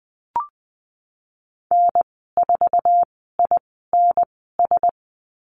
Morse code, heard as a keyed tone, about 20 wpm, spaced as for 15 wpm, 700 Hz. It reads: N4INS
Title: 062 Lesson 11g - Letter h - Call Signs 20wpm